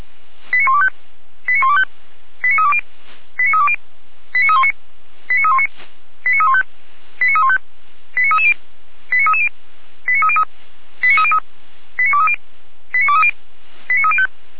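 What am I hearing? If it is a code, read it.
Five-tone selective call (ZVEI), heard as quick runs of five tones, tones of about 70 ms. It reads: 89125 89125 89320 89320 89319 89319 89315 89315 89330 89330 89303 89303 89230 89230 89305
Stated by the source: ZVEI five-tone selective call received by SDR.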